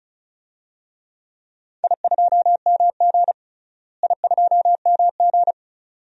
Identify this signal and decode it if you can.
Morse code, heard as a keyed tone, about 34 wpm, 700 Hz. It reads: I2MG I2MG
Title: DXCC-100 Call Signs - 4 Characters - Repeated 2x - 35wpm